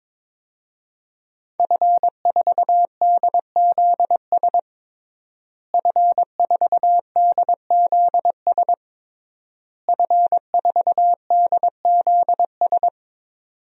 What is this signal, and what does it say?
Morse code, heard as a keyed tone, about 22 wpm, 700 Hz. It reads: F4DZS F4DZS F4DZS
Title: DXCC-100 Call Signs - 5 Characters - Repeated 3x - 22wpm